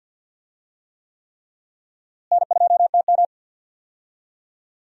Morse code, heard as a keyed tone, about 50 wpm, 700 Hz. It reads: N2TM
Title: USA Call Signs - 4 Characters - Repeated 3x - 50wpm